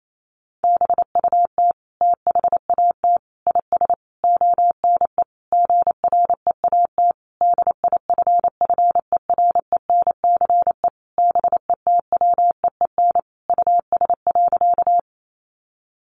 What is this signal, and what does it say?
Morse code, heard as a keyed tone, about 28 wpm, 700 Hz. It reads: BUT THAT IS ONE GREAT DIFFERENCE BETWEEN US.
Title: Sentences from Top 800 Words 28wpm